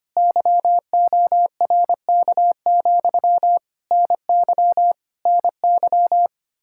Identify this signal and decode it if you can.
Morse code, heard as a keyed tone, about 25 wpm, 700 Hz. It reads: YORK, NY NY
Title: Practice QSO - Exchange 1 - Beginning - 25wpm